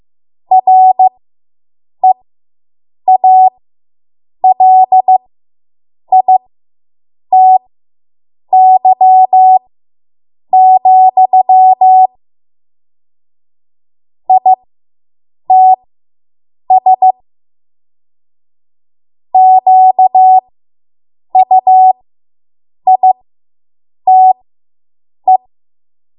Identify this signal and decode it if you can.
Morse code, heard as a keyed tone, about 15 wpm, spaced as for 7 wpm, 750 Hz. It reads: REALITY, ITS QUITE